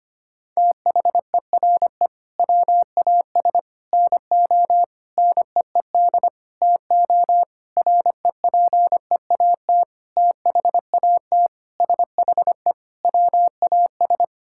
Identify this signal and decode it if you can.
Morse code, heard as a keyed tone, about 25 wpm, 700 Hz. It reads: THERE WAS NO NEED TO REPEAT THAT SHE WAS
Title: Sentences from Top 1600 Words 25wpm